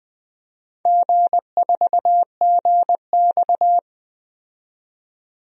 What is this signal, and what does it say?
Morse code, heard as a keyed tone, about 20 wpm, 700 Hz. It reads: G4GX
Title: DXCC-100 Call Signs - 4 Characters - 20wpm